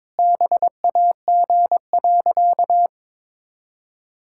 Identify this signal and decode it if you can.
Morse code, heard as a keyed tone, about 22 wpm, 700 Hz. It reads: BAG.
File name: practice / 22wpm / Sentences with 3-Word Phrases - 22wpm